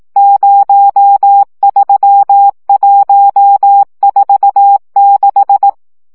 Morse code, heard as a keyed tone, about 18 wpm, 800 Hz. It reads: 03146
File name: morze vizsga 2011-3 15